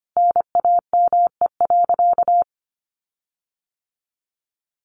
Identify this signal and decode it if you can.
Morse code, heard as a keyed tone, about 25 wpm, 700 Hz. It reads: NAME.
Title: Sentences from Top 1500 Words 25wpm